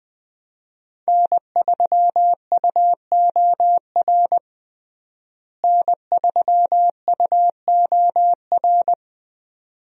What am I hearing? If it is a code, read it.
Morse code, heard as a keyed tone, about 20 wpm, 700 Hz. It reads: N3UOR N3UOR
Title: USA Call Signs - 5 Characters - Repeated 2x - 20wpm